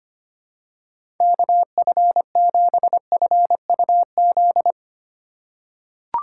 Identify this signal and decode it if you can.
Morse code, heard as a keyed tone, about 25 wpm, 700 Hz. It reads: KF7FUZ
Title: USA Call Signs - 6 Characters - 3x Character Spacing - 25wpm